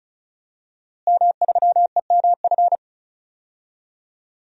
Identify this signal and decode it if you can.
Morse code, heard as a keyed tone, about 36 wpm, 700 Hz. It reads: M3EMF
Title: DXCC-100 Call Signs - 5 Characters - Repeated 3x - 35wpm